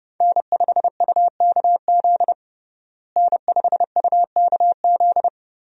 Morse code, heard as a keyed tone, about 30 wpm, 700 Hz. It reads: N5UKZ N5UKZ
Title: USA Call Signs - 5 Characters - Repeated 2x - 30wpm